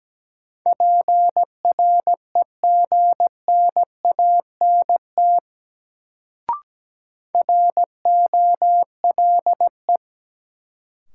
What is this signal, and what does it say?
Morse code, heard as a keyed tone, about 17 wpm, 700 Hz. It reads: PREGNANT ROLE
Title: Sentences from Top 1000 Words - Review Part 6 - New Word Drill - 17wpm